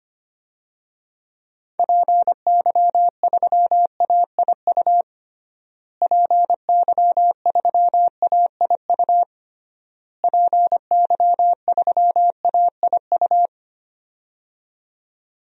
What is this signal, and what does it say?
Morse code, heard as a keyed tone, about 25 wpm, 700 Hz. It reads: PY3AIU PY3AIU PY3AIU
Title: DXCC-100 Call Signs - 6 Characters - Repeated 3x - 25wpm